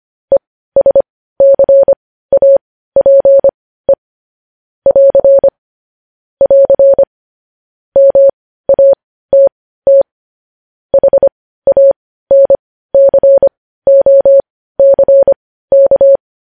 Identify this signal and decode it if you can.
Morse code, heard as a keyed tone, about 25 wpm, spaced as for 15 wpm, 550 Hz. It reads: ESCAPE <AR> <AR> MATT HANCOCK
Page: news headlines in morse code at 15 wpm